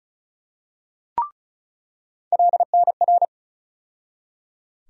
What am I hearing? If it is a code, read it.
Morse code, heard as a keyed tone, about 36 wpm, 700 Hz. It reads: LNR
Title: USA Call Sign Postfixes - 3 Letters 35wpm